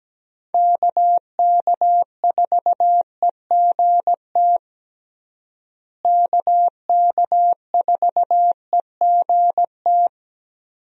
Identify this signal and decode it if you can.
Morse code, heard as a keyed tone, about 17 wpm, 700 Hz. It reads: KK4EGT KK4EGT
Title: USA Call Signs - 6 Characters - Repeated 3x - 17wpm